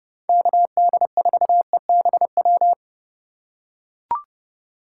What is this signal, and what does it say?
Morse code, heard as a keyed tone, about 28 wpm, 700 Hz. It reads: KD4EBW